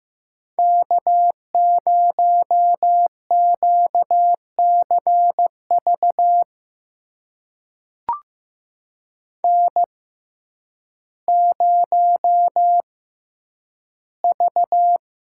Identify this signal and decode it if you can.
Morse code, heard as a keyed tone, about 15 wpm, spaced as for 9 wpm, 700 Hz. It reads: K0QCV N0V
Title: USA Call Signs - 5 Characters - 6x Character Spacing - 15wpm